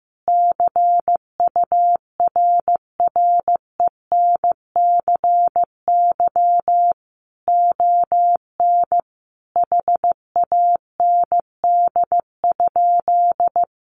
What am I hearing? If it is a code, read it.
Morse code, heard as a keyed tone, about 15 wpm, 700 Hz. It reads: CURRENCY ON HAND?
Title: Sentences from Top 1500 Words 15wpm